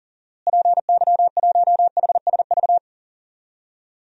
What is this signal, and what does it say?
Morse code, heard as a keyed tone, about 40 wpm, 700 Hz. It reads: PY1HSV
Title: DXCC-100 Call Signs - 6 Characters - Repeated 2x - 40wpm